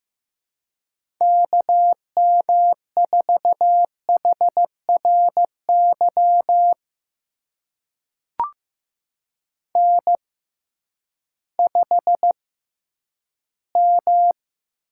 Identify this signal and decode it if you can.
Morse code, heard as a keyed tone, about 15 wpm, spaced as for 10 wpm, 700 Hz. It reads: KM4HRY N5M
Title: USA Call Signs - 6x Character Spacing - 15wpm